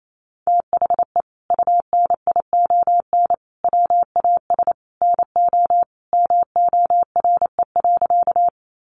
Morse code, heard as a keyed tone, about 28 wpm, 700 Hz. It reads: THE UNION WAS NO MORE.
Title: Sentences from Top 1400 Words 28wpm